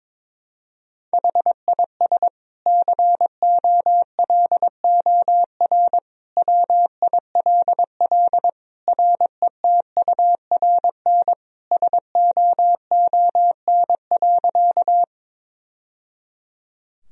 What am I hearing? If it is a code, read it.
Morse code, heard as a keyed tone, about 22 wpm, 700 Hz. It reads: HIS COLOR WILL RETURN SOON.
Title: Sentences from Top 1000 Words - Review Part 2 - 22wpm